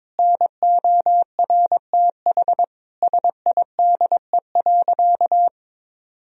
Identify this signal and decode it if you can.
Morse code, heard as a keyed tone, about 22 wpm, 700 Hz. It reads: NORTH SIDE.
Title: Sentences from Top 1000 Words - Review Part 1 - 22wpm